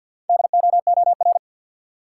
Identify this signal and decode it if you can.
Morse code, heard as a keyed tone, about 52 wpm, 700 Hz. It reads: DOOR
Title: Sentences from Top 300 Words - New Word Drill - 50wpm